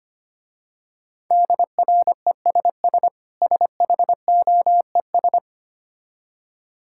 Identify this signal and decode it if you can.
Morse code, heard as a keyed tone, about 25 wpm, 700 Hz. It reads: DRESS SHOES
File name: practice / 25wpm / Difficult Words with Hints - 25wpm